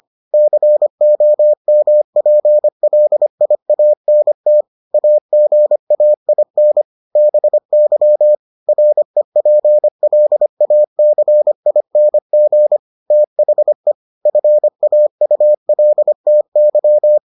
Morse code, heard as a keyed tone, about 25 wpm, 600 Hz. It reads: COMPLIANT AGAIN BY REPLACING THE FAULTY